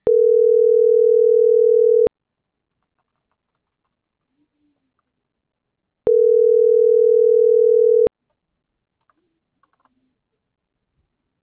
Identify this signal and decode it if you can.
Ringback tone (North America), 440 + 480 Hz.
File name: CENTRAL TERRESTRE / out karis - .77458